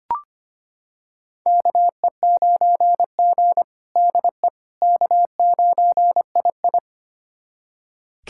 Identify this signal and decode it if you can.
Morse code, heard as a keyed tone, about 25 wpm, 700 Hz. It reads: KE9G DE K9II
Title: USA Callsign DE Callsign - 4 Characters - 25wpm